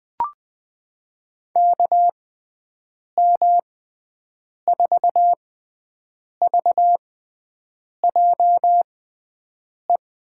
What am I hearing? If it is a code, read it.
Morse code, heard as a keyed tone, about 20 wpm, spaced as for 7 wpm, 700 Hz. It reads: KM4VJE
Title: USA Call Signs - 6 Characters - 6x Character Spacing - 20wpm